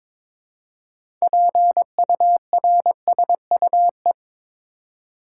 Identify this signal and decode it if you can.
Morse code, heard as a keyed tone, about 22 wpm, 700 Hz. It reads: PURSUE
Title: Six Letter Words - 22wpm